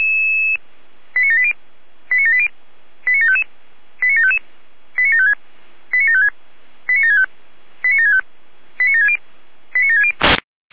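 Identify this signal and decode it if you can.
Five-tone selective call (ZVEI), heard as quick runs of five tones, tones of about 70 ms. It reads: 89780 89780 89755 89755 89756 89756 89765 89765 89760 89760
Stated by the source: ZVEI five-tone selective call 89751, received by SDR.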